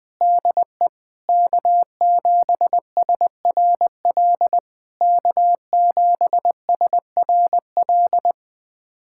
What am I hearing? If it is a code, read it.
Morse code, heard as a keyed tone, about 21 wpm, 690 Hz. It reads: DE K7SRL K7SRL
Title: SOTA QSO Practice 20wpm